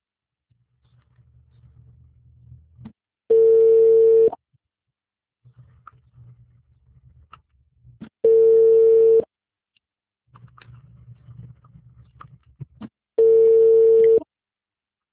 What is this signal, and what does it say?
Ringback tone (China, 450 Hz).
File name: out- -2021- - - .4